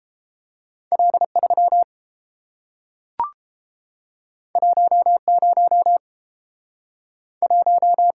Morse code, heard as a keyed tone, about 30 wpm, 700 Hz. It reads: L3 10 1